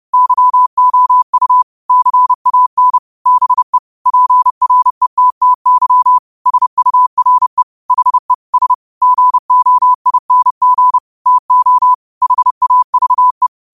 Morse code, heard as a keyed tone, about 30 wpm, 1000 Hz. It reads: YOU CAN BE PRETTY SURE HE S GOING TO HAVE
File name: qotdfiles / 30wpm